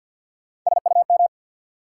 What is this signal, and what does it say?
Morse code, heard as a keyed tone, about 49 wpm, 700 Hz. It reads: SUM
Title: Sentences from Top 1000 Words - Review Part 6 - New Word Drill - 50wpm